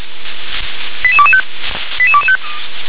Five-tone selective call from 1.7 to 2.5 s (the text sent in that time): 88225